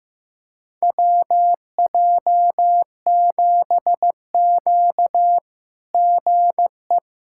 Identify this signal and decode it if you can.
Morse code, heard as a keyed tone, about 15 wpm, 700 Hz. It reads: WJ7Q GE
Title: SOTA QSO Practice 15wpm